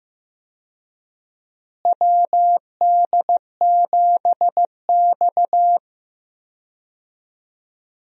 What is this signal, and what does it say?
Morse code, heard as a keyed tone, about 15 wpm, 700 Hz. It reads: WD7X
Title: USA Call Signs - 4 Characters - Repeated 2x - 15wpm